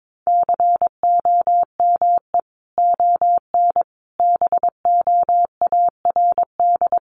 Morse code, heard as a keyed tone, about 22 wpm, 700 Hz. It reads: COME ON BOARD
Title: Sentences from Top 1300 Words 22wpm